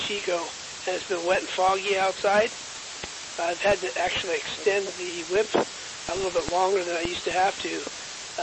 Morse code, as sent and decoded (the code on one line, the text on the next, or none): none